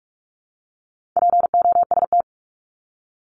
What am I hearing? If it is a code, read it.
Morse code, heard as a keyed tone, about 44 wpm, 700 Hz. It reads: POST
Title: Top 751-1000 Words 45wpm